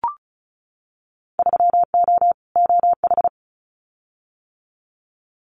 Morse code, heard as a keyed tone, about 34 wpm, 700 Hz. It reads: 3O OH
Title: Winter Field Day Exchanges 35wpm